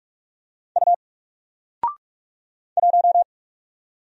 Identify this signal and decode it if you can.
Morse code, heard as a keyed tone, about 47 wpm, 700 Hz. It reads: U 1
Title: Single Character (Letter:Number:Punctuation:Prosign) Alt 45wpm